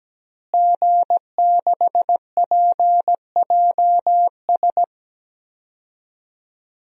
Morse code, heard as a keyed tone, about 17 wpm, 700 Hz. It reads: G6PJS
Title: DXCC-100 Call Signs - 5 Characters - Repeated 2x - 17wpm